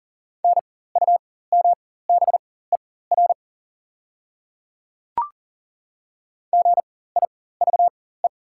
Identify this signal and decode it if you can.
Morse code, heard as a keyed tone, about 37 wpm, spaced as for 18 wpm, 700 Hz. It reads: NUMBER GIVE
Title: 4x Character Spacing - Top 200 Words - 40wpm